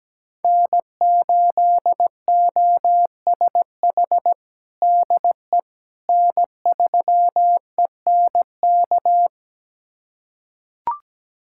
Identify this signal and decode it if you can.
Morse code, heard as a keyed tone, about 17 wpm, 700 Hz. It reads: N8OSH DE N3ENK
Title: USA Callsign DE Callsign - 5 Characters - 17wpm